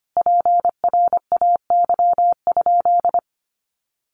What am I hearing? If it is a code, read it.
Morse code, heard as a keyed tone, about 25 wpm, 700 Hz. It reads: PRAY?
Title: A Sound Of Thunder 25wpm